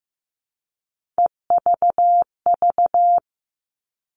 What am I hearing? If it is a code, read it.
Morse code, heard as a keyed tone, about 15 wpm, 700 Hz. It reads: EVV
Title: All Cut Numbers - 3-Digit 15wpm